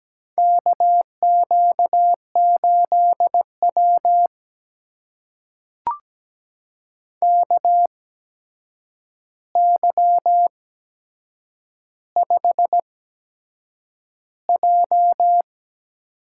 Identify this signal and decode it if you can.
Morse code, heard as a keyed tone, about 17 wpm, spaced as for 7 wpm, 700 Hz. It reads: KQ8W KY5J
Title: USA Call Signs - 4 Characters - 8x Character Spacing - 17wpm